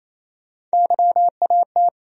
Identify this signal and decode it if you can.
Morse code, heard as a keyed tone, about 28 wpm, 700 Hz. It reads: YAT
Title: USA Call Sign Postfixes - 3 Letters 28wpm